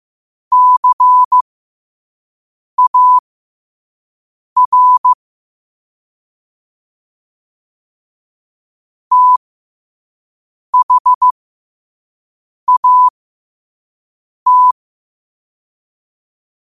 Morse code, heard as a keyed tone, about 15 wpm, spaced as for 5 wpm, 1000 Hz. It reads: CAR THAT
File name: qotdfiles / 5wpm